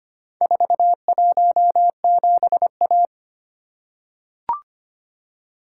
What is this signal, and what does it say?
Morse code, heard as a keyed tone, about 25 wpm, 700 Hz. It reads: 417A
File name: practice / 25wpm / Four Letter-Number Combinations - 25wpm